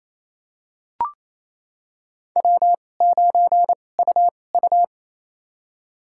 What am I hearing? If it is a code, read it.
Morse code, heard as a keyed tone, about 28 wpm, spaced as for 20 wpm, 700 Hz. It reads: W9UU
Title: USA Call Signs - 4 Characters - 2x Character Spacing - 28wpm